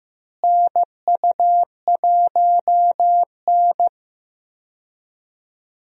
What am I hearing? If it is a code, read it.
Morse code, heard as a keyed tone, about 15 wpm, 700 Hz. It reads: NU1N
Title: USA Call Signs - 4 Characters - Repeated 3x - 15wpm